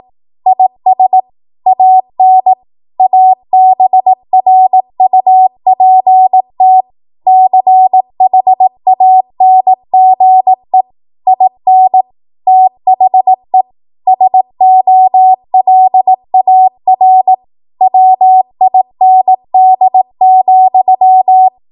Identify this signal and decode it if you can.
Morse code, heard as a keyed tone, about 18 wpm, 750 Hz. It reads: IS AN ABRUPT CHANGE IN THE SOLAR WIND,
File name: Morse / 18WPM